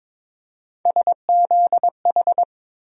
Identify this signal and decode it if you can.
Morse code, heard as a keyed tone, about 22 wpm, 700 Hz. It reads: SZH
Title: USA Call Sign Postfixes - 3 Letters 22wpm